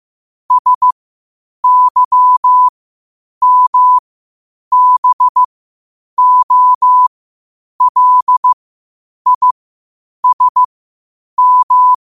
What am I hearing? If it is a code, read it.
Morse code, heard as a keyed tone, about 15 wpm, spaced as for 8 wpm, 1000 Hz. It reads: SYMBOLISM
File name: qotdfiles / 8wpm